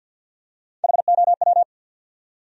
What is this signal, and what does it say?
Morse code, heard as a keyed tone, about 52 wpm, 700 Hz. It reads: HOW
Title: Top 100 Words Repeated 50wpm